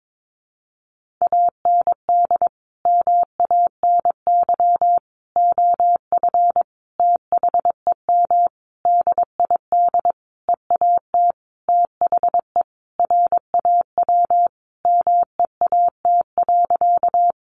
Morse code, heard as a keyed tone, about 22 wpm, 700 Hz. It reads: AND MANY OF THEM DID EAT THE RAW MEAT.
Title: Sentences from Top 1500 Words 22wpm